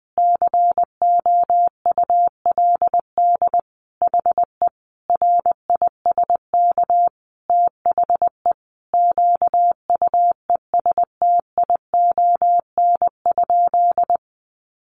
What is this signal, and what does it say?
Morse code, heard as a keyed tone, about 19 wpm, 700 Hz. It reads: COULD HE RISK THE QUESTION?
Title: Sentences from Top 600 Words 20wpm